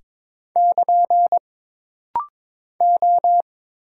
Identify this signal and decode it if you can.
Morse code, heard as a keyed tone, about 22 wpm, 700 Hz. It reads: <KN> O
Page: This rapid-fire timing is geared to push your ICR (Instant Character Recognition) skills.